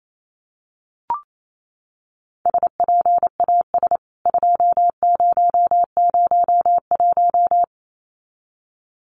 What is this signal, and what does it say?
Morse code, heard as a keyed tone, about 28 wpm, 700 Hz. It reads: SPAS 2001
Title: CWT Exchange 28wpm